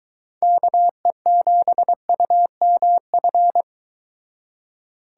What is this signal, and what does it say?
Morse code, heard as a keyed tone, about 23 wpm, 700 Hz. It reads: KE7UMF